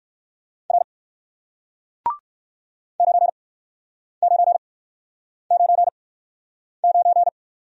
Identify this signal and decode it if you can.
Morse code, heard as a keyed tone, about 60 wpm, spaced as for 9 wpm, 700 Hz. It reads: R 9999